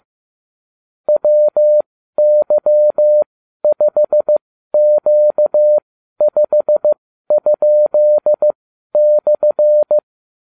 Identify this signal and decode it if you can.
Morse code, heard as a keyed tone, about 15 wpm, spaced as for 12 wpm, 600 Hz. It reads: WY5Q5?/